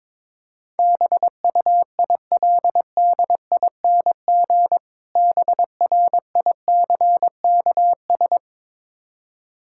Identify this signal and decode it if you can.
Morse code, heard as a keyed tone, about 22 wpm, 690 Hz. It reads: BUILDING BRICKS
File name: Difficult Words with Hints - 22wpm